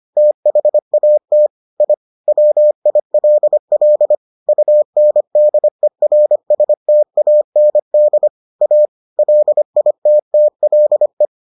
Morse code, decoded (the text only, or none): THAT I WILL UNDERSTAND A LITTLE